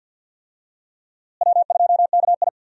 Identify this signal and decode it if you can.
Morse code, heard as a keyed tone, about 53 wpm, 700 Hz. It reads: W2KI
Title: USA Call Signs - 4 Characters - Repeated 2x - 50wpm